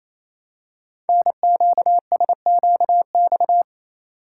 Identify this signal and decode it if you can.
Morse code, heard as a keyed tone, about 29 wpm, 700 Hz. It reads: NQSQX
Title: Code Groups - 5 Letters - 28wpm